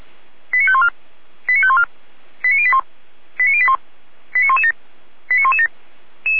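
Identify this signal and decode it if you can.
Five-tone selective call (ZVEI), heard as quick runs of five tones, tones of about 70 ms. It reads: 89514 89514 89081 89081 89117 89117
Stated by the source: Five-tone ZVEI selective call received by SDR.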